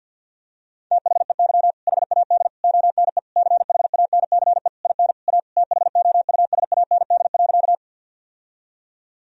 Morse code, heard as a keyed tone, about 50 wpm, 700 Hz. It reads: THEY HAD ONE CHANCE IN A THOUSAND.
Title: Sentences from Top 1000 Words - Review Part 3 - 50wpm